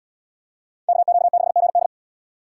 Morse code, heard as a keyed tone, about 74 wpm, 700 Hz. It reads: KQ6GD